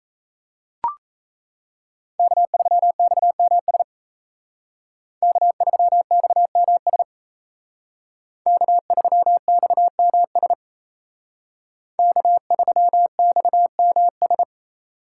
Morse code, heard as a keyed tone, about 34 wpm, 700 Hz. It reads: K3XMS K3XMS K3XMS K3XMS
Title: Speed Racing - USA 1x3 Call Signs 28wpm